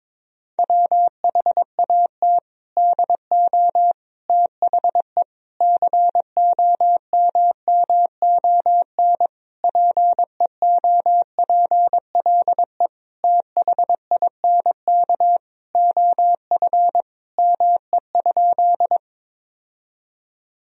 Morse code, heard as a keyed tone, about 22 wpm, 700 Hz. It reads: WHAT DO THE COMMON PEOPLE THINK OF ME?
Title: Sentences from Top 1000 Words - Review Part 3 - 22wpm